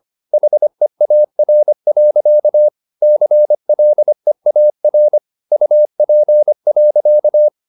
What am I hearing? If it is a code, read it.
Morse code, heard as a keyed tone, about 25 wpm, 600 Hz. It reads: HEAR. CLEAR UP.